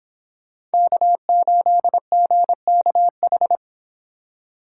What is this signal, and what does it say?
Morse code, heard as a keyed tone, about 28 wpm, 700 Hz. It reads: K8GKH